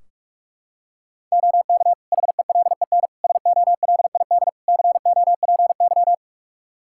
Morse code, heard as a keyed tone, about 46 wpm, 700 Hz. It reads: OK HELEN SOLID COPY
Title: Practice QSO - Exchange 3 - More Info - 45wpm